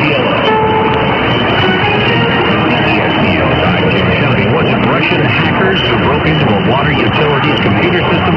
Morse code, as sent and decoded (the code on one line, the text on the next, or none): none